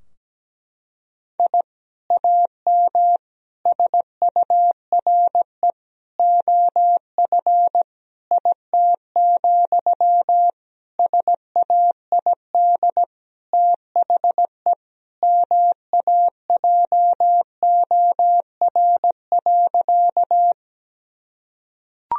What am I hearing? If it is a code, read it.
Morse code, heard as a keyed tone, about 17 wpm, 700 Hz. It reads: I AM SURE OF IT, SAID THE MAJOR.
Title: Sentences from Top 1000 Words - Review Part 4 - 17wpm